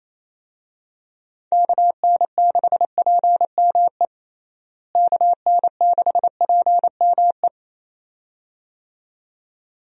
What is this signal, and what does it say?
Morse code, heard as a keyed tone, about 27 wpm, 700 Hz. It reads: KN6PME KN6PME
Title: USA Call Signs - 6 Characters - Repeated 2x - 28wpm